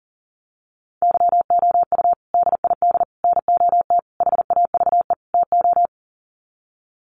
Morse code, heard as a keyed tone, about 40 wpm, 700 Hz. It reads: YOU DID NOT HAVE TO
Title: Sets of 5 Words Encore 40wpm